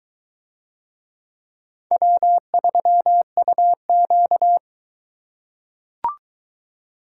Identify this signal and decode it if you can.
Morse code, heard as a keyed tone, about 21 wpm, 700 Hz. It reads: W3UQ